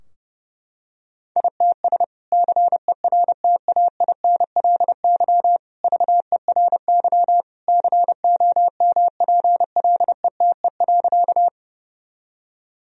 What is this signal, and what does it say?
Morse code, heard as a keyed tone, about 32 wpm, 700 Hz. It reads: ITS CERTAINLY VERY COMPLETE.
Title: Sentences from Top 1000 Words - Review Part 2 - 30wpm